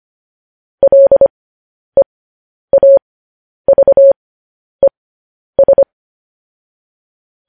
Morse code, heard as a keyed tone, about 25 wpm, spaced as for 10 wpm, 550 Hz. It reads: LEAVES
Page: News Headlines in Morse Code at 10 WPM